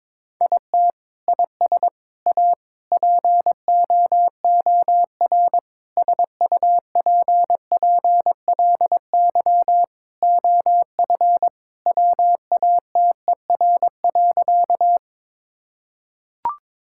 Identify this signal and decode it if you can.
Morse code, heard as a keyed tone, about 21 wpm, 700 Hz. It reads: IT IS A POOR SUPPLY OF WATER.
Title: Sentences from Top 1000 Words - Review Part 5 - 22wpm